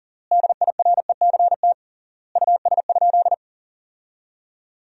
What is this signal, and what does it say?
Morse code, heard as a keyed tone, about 40 wpm, 700 Hz. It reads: DIRECT US?
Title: 3x Word-Spacing - Sentences from Top 400 Words - 40wpm